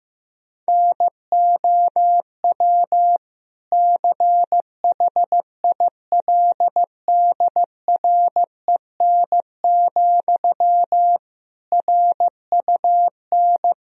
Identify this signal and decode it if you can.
Morse code, heard as a keyed tone, about 16 wpm, 700 Hz. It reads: NOW CHILDREN, RUN
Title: Sentences from Top 1000 Words - Review Part 1 - 15wpm